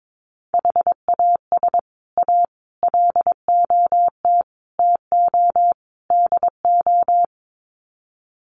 Morse code, heard as a keyed tone, about 22 wpm, 700 Hz. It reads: HAS A LOT TO DO
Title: Sets of 5 Words Encore2 22wpm